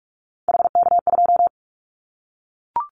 Morse code, heard as a keyed tone, about 44 wpm, 700 Hz. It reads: HK2
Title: Callsign Prefixes for Top 100 DXCC - 45wpm